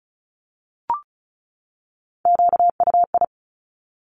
Morse code, heard as a keyed tone, about 36 wpm, 700 Hz. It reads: QUI
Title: Top Three Letter Combinations in Words 35wpm